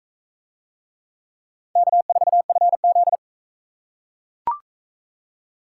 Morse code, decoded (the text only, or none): K4FZ